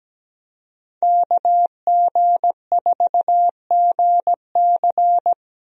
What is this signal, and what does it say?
Morse code, heard as a keyed tone, about 17 wpm, 700 Hz. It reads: KG4GC
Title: USA Call Signs - 5 Characters - Repeated 3x - 17wpm